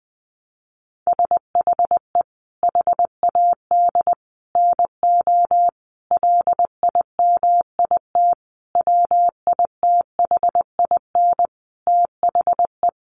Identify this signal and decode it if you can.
Morse code, heard as a keyed tone, about 20 wpm, 700 Hz. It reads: SHE HAD NO LIMIT WITHIN THE
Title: Sentences from Top 1500 Words 20wpm